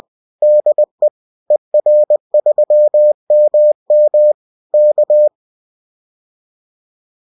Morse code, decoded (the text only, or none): DE ER3MM K